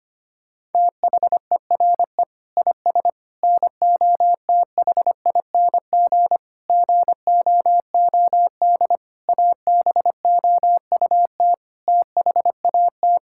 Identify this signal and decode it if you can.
Morse code, heard as a keyed tone, about 25 wpm, 700 Hz. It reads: THERE IS NOTHING GOOD ABOUT THAT
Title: Sentences from Top 1000 Words - Review Part 5 - 25wpm